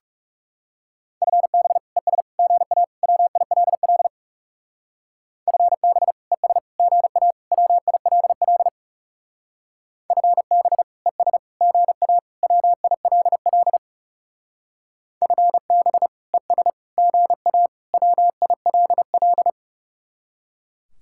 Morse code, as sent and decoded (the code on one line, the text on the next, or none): ..-. -... / . ... / --. .- / .-- .. .-.. .-.. / ..-. -... / . ... / --. .- / .-- .. .-.. .-.. / ..-. -... / . ... / --. .- / .-- .. .-.. .-.. / ..-. -... / . ... / --. .- / .-- .. .-.. .-..
FB ES GA WILL FB ES GA WILL FB ES GA WILL FB ES GA WILL